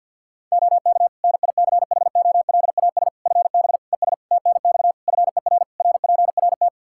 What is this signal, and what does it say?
Morse code, heard as a keyed tone, about 52 wpm, 700 Hz. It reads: OK NICHOLAS FB ES TNX FER RPRT
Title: Practice QSO - Exchange 2 - Info - 50wpm